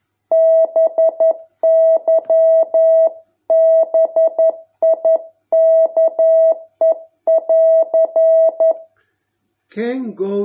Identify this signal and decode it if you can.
Morse code, heard as a keyed tone, about 11 wpm, 650 Hz. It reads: BY BIKE<AR>